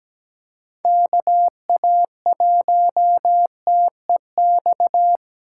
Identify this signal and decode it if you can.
Morse code, heard as a keyed tone, about 17 wpm, 700 Hz. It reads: KA1TEX
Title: USA Callsign DE Callsign 17wpm